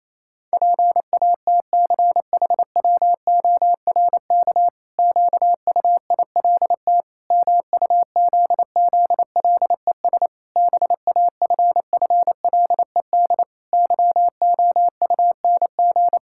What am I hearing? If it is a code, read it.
Morse code, heard as a keyed tone, about 28 wpm, 700 Hz. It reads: PATCHWORK QUILT MUZZLES BAFFLED YOUNG
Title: Pangram Challenge v2 - 28wpm